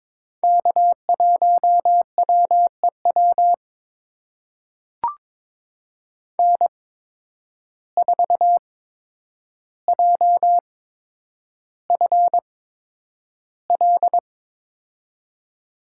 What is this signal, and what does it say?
Morse code, heard as a keyed tone, about 22 wpm, spaced as for 9 wpm, 700 Hz. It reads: K1WEW N4JFL
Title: USA Call Signs - 5 Characters - 8x Character Spacing - 22wpm